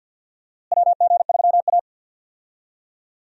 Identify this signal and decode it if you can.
Morse code, heard as a keyed tone, about 52 wpm, 700 Hz. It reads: WG3A
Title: USA Call Signs - 4 Characters - Repeated 3x - 50wpm